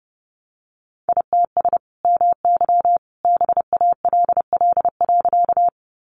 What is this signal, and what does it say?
Morse code, heard as a keyed tone, about 30 wpm, 700 Hz. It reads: ITS MY BALL.
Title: Sentences from Top 900 Words 30wpm